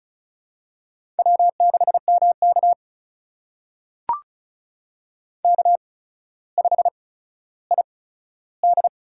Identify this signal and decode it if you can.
Morse code, heard as a keyed tone, about 36 wpm, spaced as for 15 wpm, 700 Hz. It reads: W6MK K5ID
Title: USA Call Signs - 4 Characters - 8x Character Spacing - 35wpm